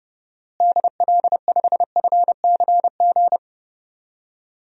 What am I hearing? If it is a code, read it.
Morse code, heard as a keyed tone, about 30 wpm, 700 Hz. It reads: DL5FCG
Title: DXCC-100 Call Signs - 6 Characters - Repeated 2x - 30wpm